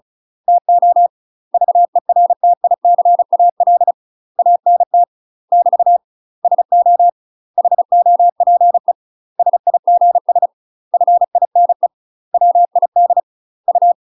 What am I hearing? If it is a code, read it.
Morse code, heard as a keyed tone, about 34 wpm, 700 Hz. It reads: TO VERTICAL ANT <BT> SO HOPE SIGS FINE WID U
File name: UK MASTER QSOs TRAINING / 35WPM / UK MASTER QSOs 0000